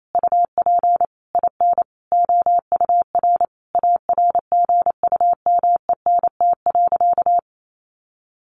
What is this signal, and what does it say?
Morse code, heard as a keyed tone, about 28 wpm, 700 Hz. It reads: UP IN OUR ARGUMENT.
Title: Sentences from Top 1500 Words 28wpm